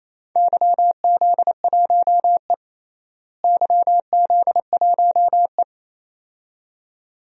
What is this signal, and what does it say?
Morse code, heard as a keyed tone, about 28 wpm, 700 Hz. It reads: YZ1E YZ1E
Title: DXCC-100 Call Signs - 4 Characters - Repeated 2x - 28wpm